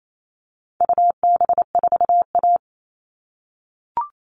Morse code, heard as a keyed tone, about 28 wpm, 700 Hz. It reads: UB4A
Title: All Cut Numbers - 4-Digit 28wpm